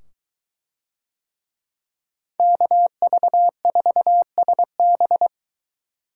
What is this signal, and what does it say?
Morse code, heard as a keyed tone, about 21 wpm, 700 Hz. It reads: KV4SB